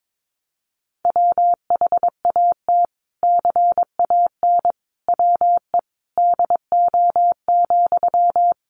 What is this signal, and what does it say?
Morse code, heard as a keyed tone, about 22 wpm, 700 Hz. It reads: WHAT CAN WE DO,
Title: Sentences from Top 100 Words 22wpm